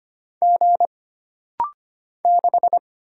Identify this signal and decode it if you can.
Morse code, heard as a keyed tone, about 25 wpm, 700 Hz. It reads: G 6
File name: Single Character (Letter:Number:Punctuation:Prosign) - Rapid-Fire - 25wpm